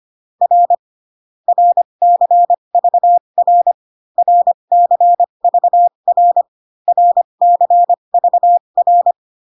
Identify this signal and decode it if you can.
Morse code, heard as a keyed tone, about 25 wpm, spaced as for 22 wpm, 700 Hz. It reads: R RCVR RCVR RCVR
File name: CW common abbreviations alphabetical repeated 3 times 25 18wpm